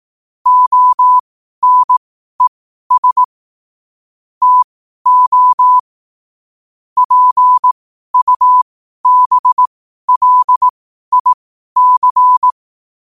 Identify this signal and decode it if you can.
Morse code, heard as a keyed tone, about 18 wpm, spaced as for 12 wpm, 1000 Hz. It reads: ONES TO PUBLIC
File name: qotdfiles / 12wpm